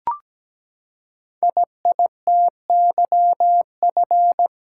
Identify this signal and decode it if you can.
Morse code, heard as a keyed tone, about 17 wpm, 700 Hz. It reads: IITYF
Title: Code Groups - 5 Letters - 17wpm